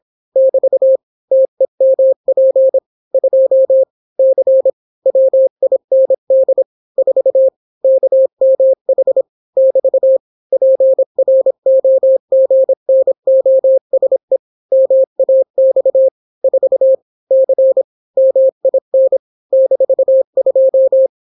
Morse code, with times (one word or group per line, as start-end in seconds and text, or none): 0.4-1.0 <BT>
1.3-2.8 TEMP
3.1-3.8 2
4.2-4.7 C
5.1-6.6 WIND
7.0-7.5 4
7.8-9.2 KMH
9.6-10.2 <BT>
10.5-14.4 PROGNOSE
14.7-16.1 MAX
16.4-16.9 4
17.3-17.8 C
18.2-19.2 MIN
19.5-21.1 -2